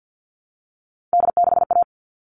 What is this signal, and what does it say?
Morse code, heard as a keyed tone, about 50 wpm, 700 Hz. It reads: D6A
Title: All Cut Numbers - 3-Digit 50wpm